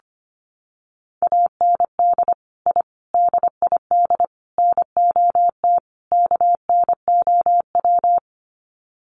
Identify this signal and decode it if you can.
Morse code, heard as a keyed tone, about 25 wpm, 700 Hz. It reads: AND I DID NOT KNOW
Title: Sets of 5 Words Encore 25wpm